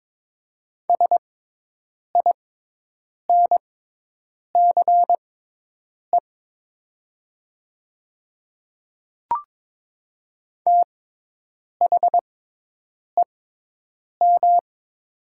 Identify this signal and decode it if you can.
Morse code, heard as a keyed tone, about 21 wpm, spaced as for 8 wpm, 700 Hz. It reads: SINCE THEM
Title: 6x Character Spacing - Top 300 Words - 22wpm